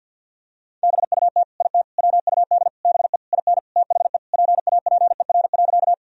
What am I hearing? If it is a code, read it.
Morse code, heard as a keyed tone, about 46 wpm, 700 Hz. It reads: BUT IT WUD BE IN THE PAPER.
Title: Sentences from Top 300 Words and Abbreviations 50wpm